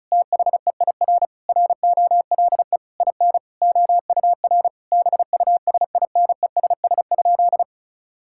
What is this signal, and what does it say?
Morse code, heard as a keyed tone, about 36 wpm, 700 Hz. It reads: THEIR ROLE IN OUR BUSINESS?
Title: Sentences from Top 1000 Words - Review Part 6 - 35wpm